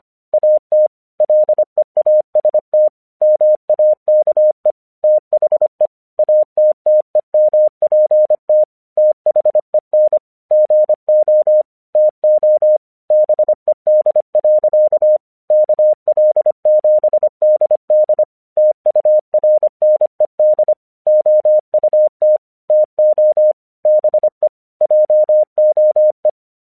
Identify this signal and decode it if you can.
Morse code, heard as a keyed tone, about 25 wpm, 610 Hz. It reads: AT LEAST MAKE THE ATTEMPT THEN GO TO BED. KL7DD TURNED OUT TO BE JOE